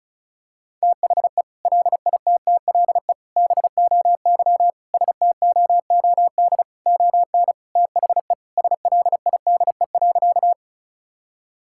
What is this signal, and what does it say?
Morse code, heard as a keyed tone, about 34 wpm, 700 Hz. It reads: THE LITTLE BOY STOOD ON THE SLIDE.
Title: Sentences from Top 1600 Words 35wpm